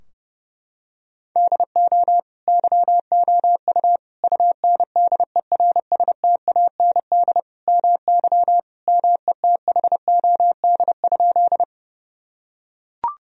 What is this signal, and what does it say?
Morse code, heard as a keyed tone, about 30 wpm, 700 Hz. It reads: DO YOU UNDERSTAND MY METHOD?
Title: Sentences from Top 1000 Words - Review Part 3 - 30wpm